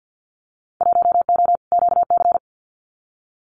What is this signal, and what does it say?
Morse code, heard as a keyed tone, about 50 wpm, 700 Hz. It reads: 1O QC
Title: Winter Field Day Exchanges 50wpm